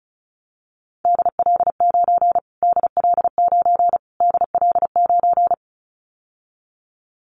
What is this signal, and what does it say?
Morse code, heard as a keyed tone, about 34 wpm, 700 Hz. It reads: DL9 DL9 DL9
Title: Callsign Prefixes for Top 25 DXCC - Repeated 3x - 35wpm